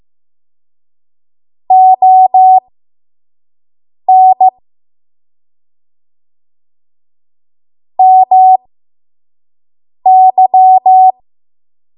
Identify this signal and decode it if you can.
Morse code, heard as a keyed tone, about 15 wpm, spaced as for 5 wpm, 750 Hz. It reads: ON MY
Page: 5 WPM Code Archive